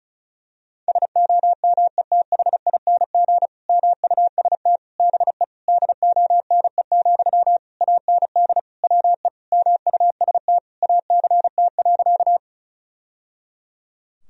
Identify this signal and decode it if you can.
Morse code, heard as a keyed tone, about 36 wpm, 700 Hz. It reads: SOMETHING MUST BE DONE, AND WE MUST ACT.
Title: Sentences from Top 1000 Words - Review Part 1 - 35wpm